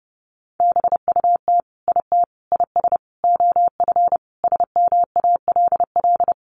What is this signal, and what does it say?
Morse code, heard as a keyed tone, about 30 wpm, 700 Hz. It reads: BUT IT IS OF SMALL
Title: Sentences from Top 1200 Words 30wpm